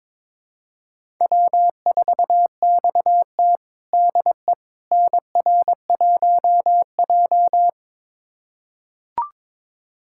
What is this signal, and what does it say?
Morse code, heard as a keyed tone, about 21 wpm, 700 Hz. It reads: W4XT DE NR1J
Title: USA Callsign DE Callsign - 4 Characters - 22wpm